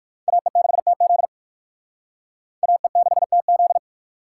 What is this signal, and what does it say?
Morse code, heard as a keyed tone, about 60 wpm, 700 Hz. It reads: AE6TZ AE6TZ